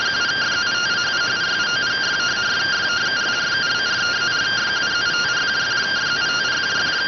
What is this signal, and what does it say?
Two-tone teleprinter signal, about 170 Hz shift.